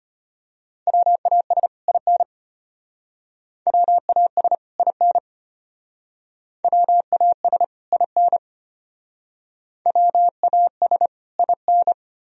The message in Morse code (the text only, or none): WAS IN WAS IN WAS IN WAS IN